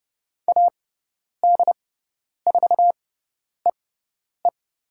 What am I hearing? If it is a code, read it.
Morse code, heard as a keyed tone, about 30 wpm, spaced as for 10 wpm, 700 Hz. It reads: AD4EE
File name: Lesson 17g - Letter u - Call Signs 30wpm